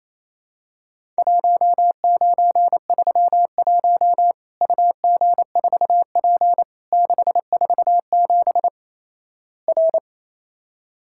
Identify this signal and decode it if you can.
Morse code, heard as a keyed tone, about 27 wpm, 700 Hz. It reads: 1931 UG4P 647 R
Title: Worked All Europe DX Contest - QTC Exchange - 28wpm